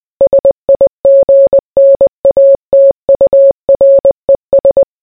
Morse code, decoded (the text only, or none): SIGNATURES